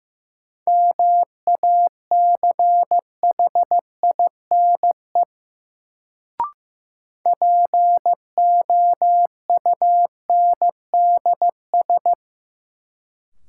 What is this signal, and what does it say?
Morse code, heard as a keyed tone, about 15 wpm, 700 Hz. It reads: MACHINE POUNDS
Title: Sentences from Top 1000 Words - Review Part 2 - New Word Drill - 15wpm